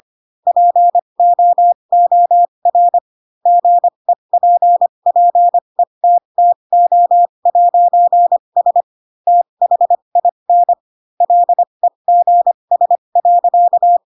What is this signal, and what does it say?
Morse code, heard as a keyed tone, about 25 wpm, spaced as for 22 wpm, 700 Hz. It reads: POOR GEPPETTO'S THIN LEGS.
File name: Pinocchio / Pinocchio ch2 18 25 ik1hge